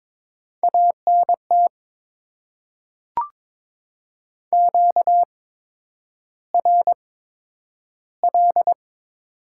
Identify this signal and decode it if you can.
Morse code, heard as a keyed tone, about 22 wpm, spaced as for 9 wpm, 700 Hz. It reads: ANT QRL
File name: Top 20 QSO Elements - 8x Character Spacing - 22wpm